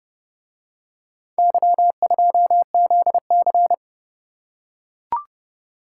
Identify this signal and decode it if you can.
Morse code, heard as a keyed tone, about 28 wpm, 700 Hz. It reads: Y2ZC